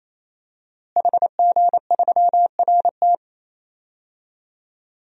Morse code, heard as a keyed tone, about 28 wpm, 700 Hz. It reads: HG3RT
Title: DXCC-100 Call Signs - 5 Characters - Repeated 3x - 28wpm